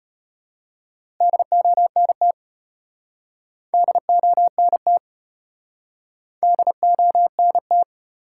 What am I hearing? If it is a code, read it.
Morse code, heard as a keyed tone, about 34 wpm, 700 Hz. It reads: DONT DONT DONT